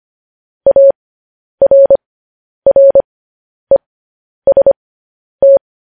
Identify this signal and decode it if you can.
Morse code, heard as a keyed tone, about 25 wpm, spaced as for 10 wpm, 550 Hz. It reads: ARREST
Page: News Headlines in Morse Code at 10 WPM